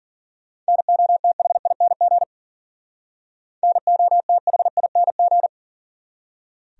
Morse code, heard as a keyed tone, about 52 wpm, 700 Hz. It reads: NOTHING NOTHING